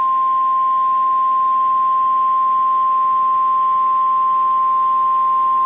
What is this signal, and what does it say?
Emergency Alert System (SAME) tones (the 1050 Hz attention tone). It is heard from 0.2 to 5.3 s.